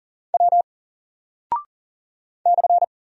Morse code, heard as a keyed tone, about 40 wpm, 700 Hz. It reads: W /
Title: Single Character (Letter:Number:Punctuation:Prosign) Alt 40wpm